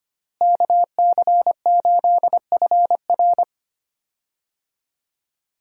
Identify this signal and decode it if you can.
Morse code, heard as a keyed tone, about 25 wpm, 700 Hz. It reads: KC8FR
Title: USA Call Signs - 5 Characters - Repeated 2x - 25wpm